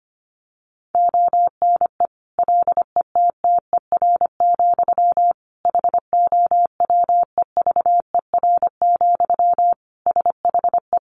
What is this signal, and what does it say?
Morse code, heard as a keyed tone, about 25 wpm, 700 Hz. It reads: ONE LETTER, HOWEVER, SHE
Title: Sentences from Top 1200 Words 25wpm